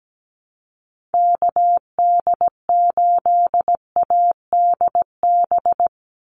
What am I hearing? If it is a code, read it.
Morse code, heard as a keyed tone, about 17 wpm, 700 Hz. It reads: KD8ADB
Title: USA Call Signs 17wpm